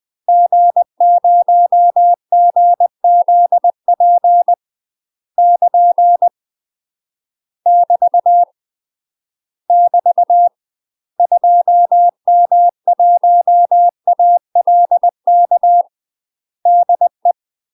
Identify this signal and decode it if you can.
Morse code, heard as a keyed tone, about 20 wpm, 700 Hz. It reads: G0GZP <KN> <BT> <BT> 2M1ALK DE
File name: MASTER QSOs TRAINING/20WPM